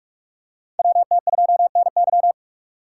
Morse code, heard as a keyed tone, about 45 wpm, 700 Hz. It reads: WT2NY
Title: USA Call Signs - 5 Characters - 3x Character Spacing - 45wpm